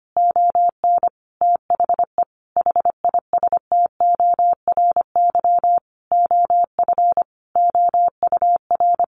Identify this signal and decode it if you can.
Morse code, heard as a keyed tone, about 25 wpm, 700 Hz. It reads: ON THE HISTORY OF OUR
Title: Sentences from Top 1500 Words 25wpm